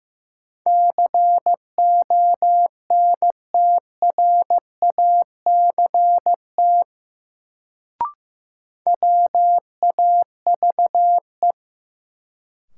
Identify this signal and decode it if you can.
Morse code, heard as a keyed tone, about 15 wpm, 700 Hz. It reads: CONTRACT WAVE
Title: Sentences from Top 1400 Words - New Word Drill - 15wpm